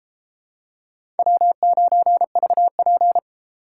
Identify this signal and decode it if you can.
Morse code, heard as a keyed tone, about 31 wpm, 700 Hz. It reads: W9VP